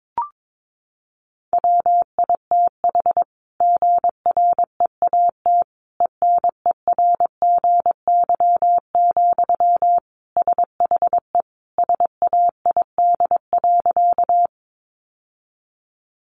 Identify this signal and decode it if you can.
Morse code, heard as a keyed tone, about 22 wpm, 700 Hz. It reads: WITH GREAT ENERGY, SHE SAID.
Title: Sentences from Top 900 Words 22wpm